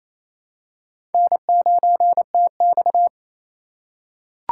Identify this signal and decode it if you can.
Morse code, heard as a keyed tone, about 29 wpm, 700 Hz. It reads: N9TX
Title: USA Call Signs - 4 Characters - 2x Character Spacing - 28wpm